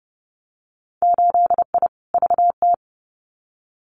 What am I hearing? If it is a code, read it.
Morse code, heard as a keyed tone, about 30 wpm, 700 Hz. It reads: 8I VT
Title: Winter Field Day Exchanges Encore 30wpm